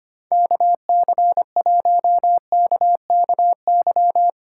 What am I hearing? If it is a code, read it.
Morse code, heard as a keyed tone, about 32 wpm, 700 Hz. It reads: KC1KKY